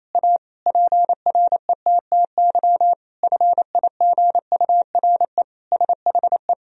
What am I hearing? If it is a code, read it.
Morse code, heard as a keyed tone, about 27 wpm, 700 Hz. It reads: A PRETTY FIGURE SHE
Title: Sentences from Top 1000 Words - Review Part 2 - 28wpm